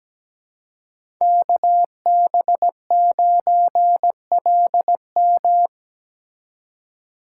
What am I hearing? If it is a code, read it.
Morse code, heard as a keyed tone, about 17 wpm, 700 Hz. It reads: KB9LM
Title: USA Call Signs - 5 Characters - 2x Character Spacing - 17wpm